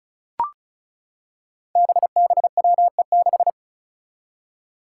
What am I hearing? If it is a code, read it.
Morse code, heard as a keyed tone, about 36 wpm, 700 Hz. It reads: BBWE6
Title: Five Challenging Letter-Number Combinations 35wpm